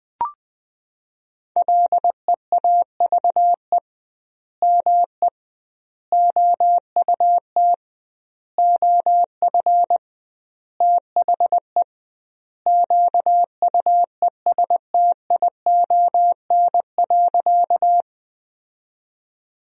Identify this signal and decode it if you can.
Morse code, heard as a keyed tone, about 20 wpm, 700 Hz. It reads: LEAVE ME OUT OF THE QUESTION.
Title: 2x Word-Spacing - Sentences from Top 400 Words - 20wpm